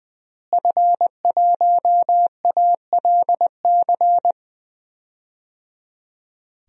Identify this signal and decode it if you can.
Morse code, heard as a keyed tone, about 20 wpm, 700 Hz. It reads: F1ALC
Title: DXCC-100 Call Signs - 5 Characters - Repeated 3x - 20wpm